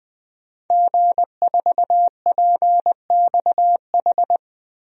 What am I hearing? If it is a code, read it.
Morse code, heard as a keyed tone, about 21 wpm, 700 Hz. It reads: G4PXH
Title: DXCC-100 Call Signs - 5 Characters - 20wpm